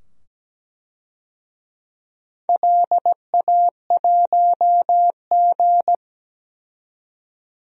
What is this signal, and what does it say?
Morse code, heard as a keyed tone, about 17 wpm, 700 Hz. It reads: LA1G
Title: DXCC-100 Call Signs - 4 Characters - 17wpm